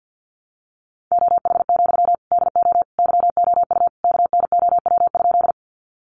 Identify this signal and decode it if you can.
Morse code, heard as a keyed tone, about 50 wpm, 700 Hz. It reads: OH, DO YOU KNOW?
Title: Sentences from Top 500 Words 50wpm